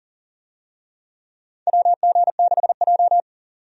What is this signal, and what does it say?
Morse code, heard as a keyed tone, about 40 wpm, 700 Hz. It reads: WG6J
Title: USA Call Signs - 4 Characters - 40wpm